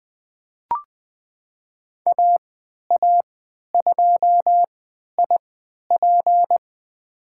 Morse code, heard as a keyed tone, about 20 wpm, spaced as for 11 wpm, 700 Hz. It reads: AA2IP
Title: USA Call Signs - 5 Characters - 3x Character Spacing - 20wpm